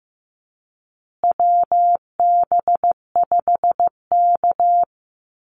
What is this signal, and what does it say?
Morse code, heard as a keyed tone, about 15 wpm, 700 Hz. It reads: WB5K
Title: Call Signs Repeated 2x - 15wpm